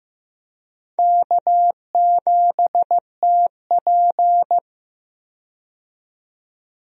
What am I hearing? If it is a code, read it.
Morse code, heard as a keyed tone, about 15 wpm, 700 Hz. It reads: K7TP
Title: USA Call Signs - 4 Characters - 15wpm